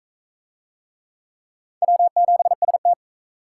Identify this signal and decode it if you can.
Morse code, heard as a keyed tone, about 44 wpm, 700 Hz. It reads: W7ST